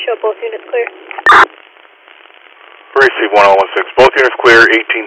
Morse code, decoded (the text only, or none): none